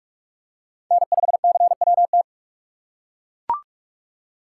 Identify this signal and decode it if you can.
Morse code, heard as a keyed tone, about 42 wpm, 700 Hz. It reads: N5CWT